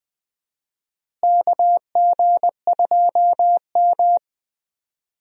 Morse code, heard as a keyed tone, about 20 wpm, 700 Hz. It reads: KG2M
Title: USA Call Signs - 4 Characters - Repeated 2x - 20wpm